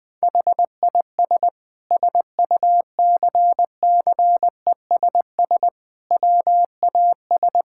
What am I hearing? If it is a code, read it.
Morse code, heard as a keyed tone, about 21 wpm, 700 Hz. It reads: HIS SUCCESS WAS
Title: Sentences from Top 1000 Words - Review Part 6 - 20wpm